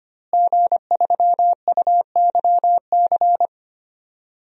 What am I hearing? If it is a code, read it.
Morse code, heard as a keyed tone, about 25 wpm, 700 Hz. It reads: G3UYC
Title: DXCC-100 Call Signs - 5 Characters - Repeated 3x - 25wpm